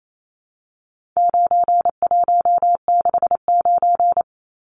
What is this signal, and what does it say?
Morse code, heard as a keyed tone, about 28 wpm, 700 Hz. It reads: 9169
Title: Numbers - 4-Digit 28wpm